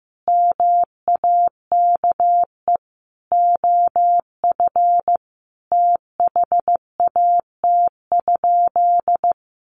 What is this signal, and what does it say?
Morse code, heard as a keyed tone, about 15 wpm, 700 Hz. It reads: MAKE OF THAT?
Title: Sentences from Top 100 Words 15wpm